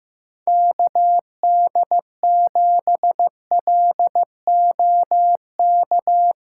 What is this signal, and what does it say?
Morse code, heard as a keyed tone, about 16 wpm, 700 Hz. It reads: KD7LOK
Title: USA Callsign DE Callsign 15wpm